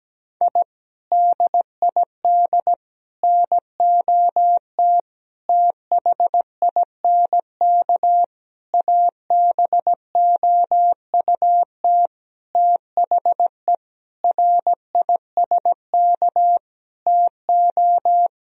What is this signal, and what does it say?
Morse code, heard as a keyed tone, about 17 wpm, 700 Hz. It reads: I DID NOT THINK ABOUT THE RISK TO
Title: Sentences from Top 1000 Words - Review Part 4 - 17wpm